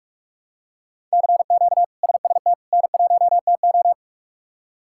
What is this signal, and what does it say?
Morse code, heard as a keyed tone, about 45 wpm, 700 Hz. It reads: CQ SST N1TO
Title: K1USN SST QSO Practice - 45wpm